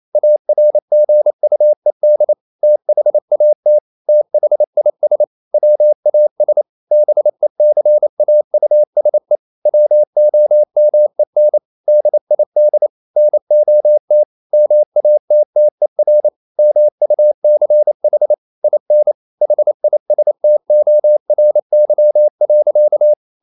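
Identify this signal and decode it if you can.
Morse code, heard as a keyed tone, about 28 wpm, 600 Hz. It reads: ARGUED THAT THIS WAS BECAUSE WOMEN DID NOT MATTER MUCH IN HISTORY.
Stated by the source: Morse code practice at 28 wpm, text of Ken Follett's novel Never in English, from CW med Gnister.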